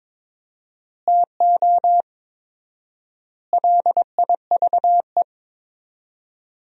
Morse code, heard as a keyed tone, about 22 wpm, 700 Hz. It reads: TO LIVE
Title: 4x Word-Spacing - Sentences from Top 200 Words - 22wpm